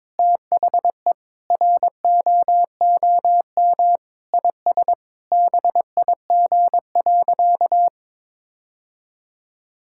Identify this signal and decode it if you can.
Morse code, heard as a keyed tone, about 22 wpm, 700 Hz. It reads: THE ROOM IS BIG.
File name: practice / 22wpm / Sentences with 3-Word Phrases - 22wpm